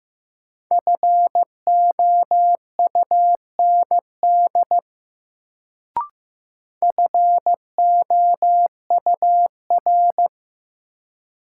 Morse code, heard as a keyed tone, about 15 wpm, 700 Hz. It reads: FOUND FOUR
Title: Sentences from Top 300 Words - New Word Drill - 15wpm